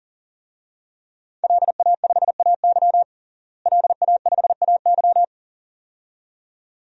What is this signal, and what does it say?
Morse code, heard as a keyed tone, about 38 wpm, 700 Hz. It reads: LA5AY LA5AY
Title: DXCC-100 Call Signs - 5 Characters - Repeated 2x - 40wpm